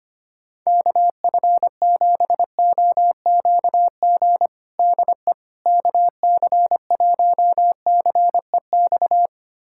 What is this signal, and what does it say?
Morse code, heard as a keyed tone, about 25 wpm, 700 Hz. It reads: KF7OQG DE KC1CEX
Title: USA Callsign DE Callsign - 6 Characters - 25wpm